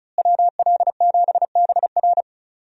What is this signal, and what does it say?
Morse code, heard as a keyed tone, about 36 wpm, 700 Hz. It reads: WL7BR
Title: USA Call Signs - 5 Characters - Repeated 3x - 35wpm